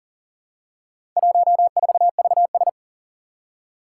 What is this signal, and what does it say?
Morse code, heard as a keyed tone, about 42 wpm, 700 Hz. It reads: 14VS
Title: Four Challenging Letter-Number Combinations 40wpm